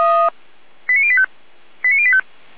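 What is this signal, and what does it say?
Five-tone selective call (ZVEI), heard as quick runs of five tones, tones of about 70 ms. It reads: 89085 89085
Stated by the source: ZVEI five-tone selective call received on an SDR.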